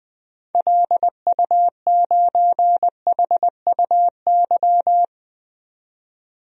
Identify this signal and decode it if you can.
Morse code, heard as a keyed tone, about 21 wpm, 700 Hz. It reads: LU9HUY
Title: DXCC-100 Call Signs - 6 Characters - 20wpm